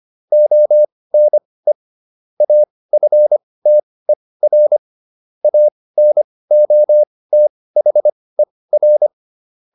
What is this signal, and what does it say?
Morse code, heard as a keyed tone, about 25 wpm, spaced as for 18 wpm, 600 Hz. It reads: ONE AFTER ANOTHER